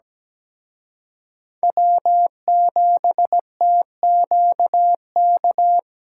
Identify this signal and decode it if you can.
Morse code, heard as a keyed tone, about 21 wpm, 700 Hz. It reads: W7TQK
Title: Speed Racing - USA 1x3 Call Signs 17wpm